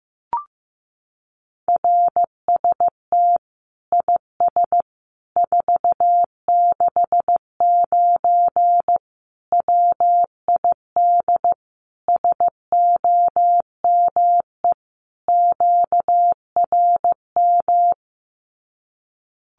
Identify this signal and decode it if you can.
Morse code, heard as a keyed tone, about 15 wpm, 700 Hz. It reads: RST IS 469 WID SOME QRM
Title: Common QSO Phrases 15wpm